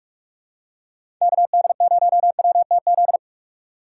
Morse code, heard as a keyed tone, about 46 wpm, 700 Hz. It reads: KD0WTZ
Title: USA Call Signs - 6 Characters - 4x Character Spacing - 45wpm